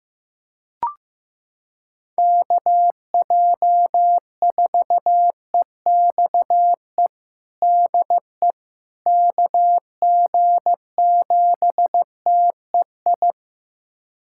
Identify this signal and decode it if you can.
Morse code, heard as a keyed tone, about 15 wpm, 700 Hz. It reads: KJ4EXE DE KG7TEI
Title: USA Callsign DE Callsign - 6 Characters - 15wpm